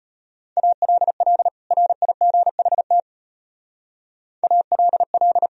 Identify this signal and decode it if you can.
Morse code, heard as a keyed tone, about 32 wpm, 700 Hz. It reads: ALL RIGHT ALL